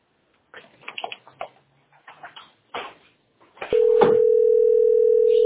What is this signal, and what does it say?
Ringback tone (North America), 440 + 480 Hz.